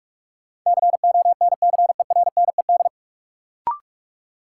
Morse code, heard as a keyed tone, about 45 wpm, 700 Hz. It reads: CONCERNED
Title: Sentences from Top 1000 Words - Review Part 6 - New Word Drill - 45wpm